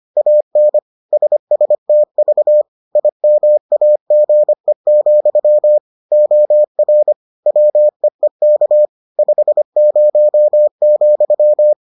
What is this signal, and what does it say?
Morse code, heard as a keyed tone, about 25 wpm, 600 Hz. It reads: AN SSTV IMAGE, OR WEEK 50,